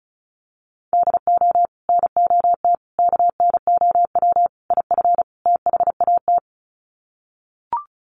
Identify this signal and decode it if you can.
Morse code, heard as a keyed tone, about 34 wpm, 700 Hz. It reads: DO NOT KNOW IF THAT
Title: Sets of 5 Words Encore2 35wpm